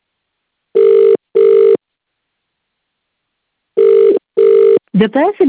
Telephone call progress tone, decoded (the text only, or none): ringback tone